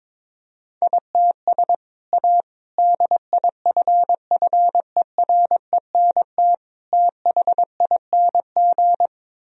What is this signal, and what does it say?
Morse code, heard as a keyed tone, about 22 wpm, 700 Hz. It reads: ITS A DIFFERENT THING
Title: Sentences from Top 1000 Words - Review Part 3 - 22wpm